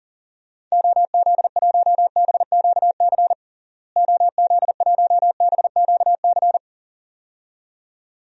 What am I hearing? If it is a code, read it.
Morse code, heard as a keyed tone, about 40 wpm, 700 Hz. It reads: OZ1BQC OZ1BQC
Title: DXCC-100 Call Signs - 6 Characters - Repeated 2x - 40wpm